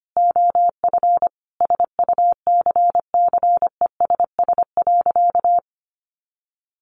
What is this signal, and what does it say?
Morse code, heard as a keyed tone, about 25 wpm, 700 Hz. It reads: OF SUCCESS.
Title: Sentences from Top 1300 Words 25wpm